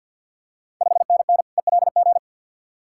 Morse code, heard as a keyed tone, about 52 wpm, 690 Hz. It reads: 5NN ELG
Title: Ontario QSO Party Exchange 50wpm